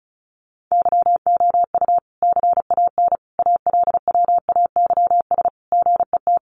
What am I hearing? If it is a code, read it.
Morse code, heard as a keyed tone, about 34 wpm, 700 Hz. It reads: YOU CAN ALWAYS GET
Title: Sentences from Top 1200 Words 35wpm